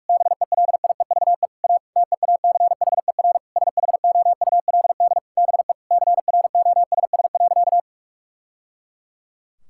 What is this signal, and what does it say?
Morse code, heard as a keyed tone, about 47 wpm, 700 Hz. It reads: BELIEVE A TEACHER SHOULD BE CROSS.
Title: Sentences from Top 1000 Words - Review Part 6 - 45wpm